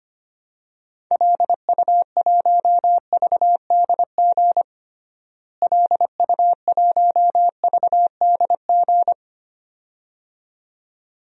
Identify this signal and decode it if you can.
Morse code, heard as a keyed tone, about 25 wpm, 700 Hz. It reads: LU1VDG LU1VDG
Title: DXCC-100 Call Signs - 6 Characters - Repeated 2x - 25wpm